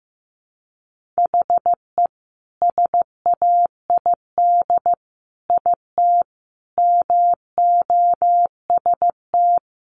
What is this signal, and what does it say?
Morse code, heard as a keyed tone, about 15 wpm, 700 Hz. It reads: HE SAID IT MOST
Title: Sentences from Top 1200 Words 15wpm